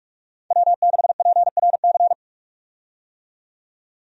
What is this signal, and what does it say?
Morse code, heard as a keyed tone, about 43 wpm, 700 Hz. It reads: W6PRC
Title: USA Call Signs - 5 Characters - 45wpm